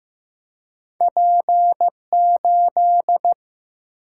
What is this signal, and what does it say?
Morse code, heard as a keyed tone, about 15 wpm, 700 Hz. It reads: P8
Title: Single-Letter Single-Number 15wpm